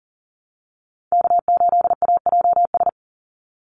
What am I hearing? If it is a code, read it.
Morse code, heard as a keyed tone, about 37 wpm, 700 Hz. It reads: K8AJS
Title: Call Signs - 40wpm